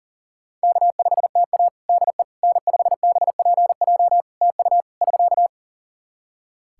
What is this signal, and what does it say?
Morse code, heard as a keyed tone, about 37 wpm, 700 Hz. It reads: K5TA DE N5BPJ TU <SK>
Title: K1USN SST QSO Practice 40wpm